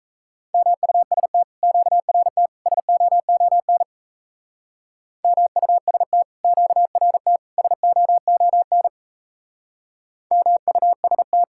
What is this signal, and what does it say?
Morse code, heard as a keyed tone, about 38 wpm, 700 Hz. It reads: MUST QRT SOON MUST QRT SOON MUST